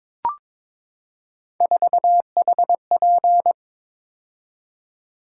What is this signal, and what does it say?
Morse code, heard as a keyed tone, about 22 wpm, 700 Hz. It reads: 4HP
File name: practice / 22wpm / Three Letter-Number Combinations - 22wpm